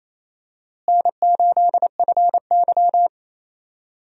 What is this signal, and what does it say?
Morse code, heard as a keyed tone, about 28 wpm, 700 Hz. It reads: N8FY
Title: USA Call Signs - 4 Characters - Repeated 3x - 28wpm